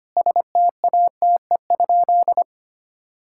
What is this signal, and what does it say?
Morse code, heard as a keyed tone, about 25 wpm, 700 Hz. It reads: STATE?
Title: Sentences from Top 1000 Words - Review Part 5 - 25wpm